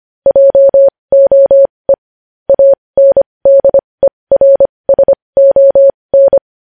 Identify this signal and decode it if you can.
Morse code, heard as a keyed tone, about 25 wpm, spaced as for 20 wpm, 550 Hz. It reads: JOE ANDERSON
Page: news-headlines-in-morse-code-at-20-wpm